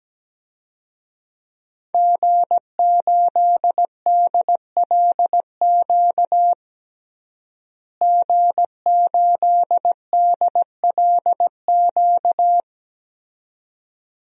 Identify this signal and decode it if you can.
Morse code, heard as a keyed tone, about 17 wpm, 700 Hz. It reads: G8DLQ G8DLQ
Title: DXCC-100 Call Signs - 5 Characters - Repeated 2x - 17wpm